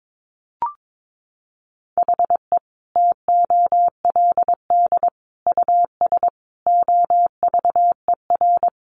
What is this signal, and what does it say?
Morse code, heard as a keyed tone, about 22 wpm, 700 Hz. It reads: HE TOLD US OVER
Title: Sentences from Top 1100 Words 22wpm